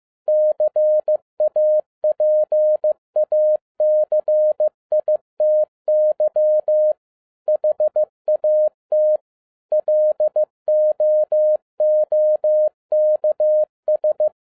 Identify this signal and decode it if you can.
Morse code, heard as a keyed tone, about 15 wpm, 610 Hz. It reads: CAPACITY HAT LOOKS
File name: 2012-01-mp3 / n07-ARA JR0DIL-15wpm